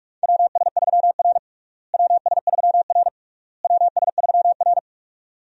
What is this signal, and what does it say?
Morse code, heard as a keyed tone, about 41 wpm, 700 Hz. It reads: WS3R WS3R WS3R
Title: USA Call Signs - 4 Characters - Repeated 3x - 45wpm